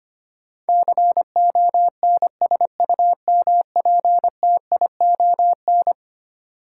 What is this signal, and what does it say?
Morse code, heard as a keyed tone, about 25 wpm, 700 Hz. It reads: CONSUMPTION
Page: morse-code-ninja-practice-25wpm